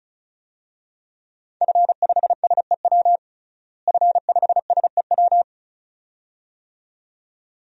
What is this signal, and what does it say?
Morse code, heard as a keyed tone, about 36 wpm, 700 Hz. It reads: F5SEW F5SEW
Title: DXCC-100 Call Signs - 5 Characters - Repeated 2x - 35wpm